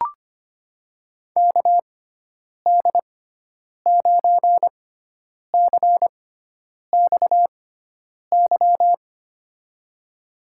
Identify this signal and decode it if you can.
Morse code, heard as a keyed tone, about 25 wpm, spaced as for 9 wpm, 700 Hz. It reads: KD9CXY
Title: USA Call Signs - 6 Characters - 6x Character Spacing - 25wpm